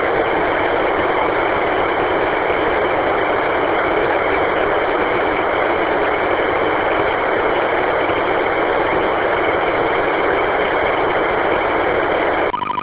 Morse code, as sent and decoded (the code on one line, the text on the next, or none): none